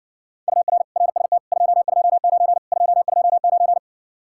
Morse code, heard as a keyed tone, about 60 wpm, 700 Hz. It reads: UR RST 229 229